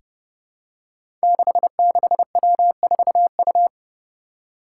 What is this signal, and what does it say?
Morse code, heard as a keyed tone, about 32 wpm, 700 Hz. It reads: 66W4U
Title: Five Challenging Letter-Number Combinations 30wpm